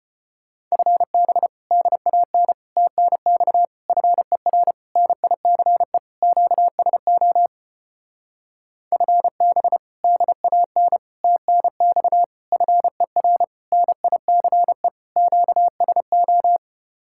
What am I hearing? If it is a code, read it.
Morse code, heard as a keyed tone, about 33 wpm, 700 Hz. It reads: FB DAN TNX FER NICE QSO FB DAN TNX FER NICE QSO